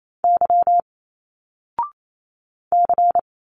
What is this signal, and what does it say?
Morse code, heard as a keyed tone, about 28 wpm, 700 Hz. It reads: Y C
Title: Single Letters 28wpm